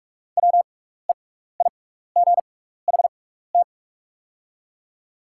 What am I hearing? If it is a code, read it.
Morse code, heard as a keyed tone, about 44 wpm, spaced as for 15 wpm, 690 Hz. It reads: WEIGHT
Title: 6x Character Spacing - Top 500 Words - 45wpm